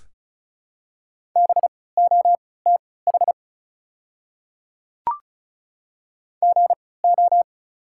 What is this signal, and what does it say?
Morse code, heard as a keyed tone, about 34 wpm, spaced as for 20 wpm, 700 Hz. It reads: BOTH GO
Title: 3x Character Spacing - Top 300 Words - 35wpm